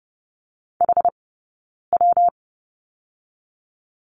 Morse code, heard as a keyed tone, about 30 wpm, 700 Hz. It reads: H W
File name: Lesson 19d - Letter w - Two Characters 30wpm